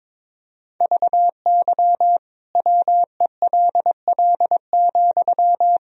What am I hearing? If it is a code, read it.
Morse code, heard as a keyed tone, about 22 wpm, 700 Hz. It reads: VY WELL,
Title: Sentences from Top 500 Words and Abbreviations 22wpm